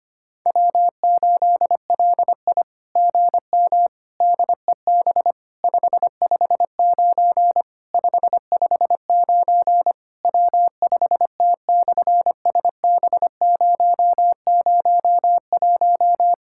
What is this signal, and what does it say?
Morse code, heard as a keyed tone, about 25 wpm, 700 Hz. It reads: W8LI GM DEB 559 559 W5T/SB001
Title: SOTA QSO Practice 25wpm